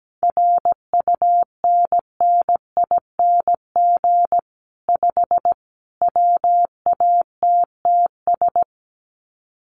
Morse code, heard as a keyed tone, about 17 wpm, 700 Hz. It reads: RUNNING 5 WATTS
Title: Common QSO Phrases 17wpm